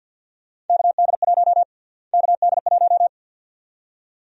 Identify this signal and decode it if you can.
Morse code, heard as a keyed tone, about 48 wpm, 700 Hz. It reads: KD1 KD1
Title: USA Call Sign Prefixes - Repeated 2x - 50wpm